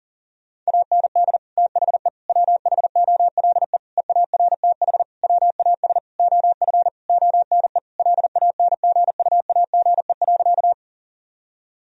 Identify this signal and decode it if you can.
Morse code, heard as a keyed tone, about 37 wpm, 700 Hz. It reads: AND THE WHOLE EARTH WAS OF ONE LANGUAGE.
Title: Sentences from Top 1000 Words - Review Part 3 - 40wpm